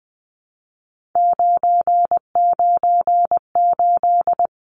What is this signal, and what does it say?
Morse code, heard as a keyed tone, about 20 wpm, 700 Hz. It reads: 998
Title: Numbers - 3-Digit 20wpm